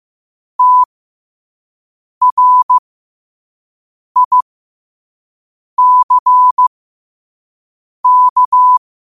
Morse code, heard as a keyed tone, about 15 wpm, spaced as for 5 wpm, 1000 Hz. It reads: TRICK
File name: qotdfiles / 5wpm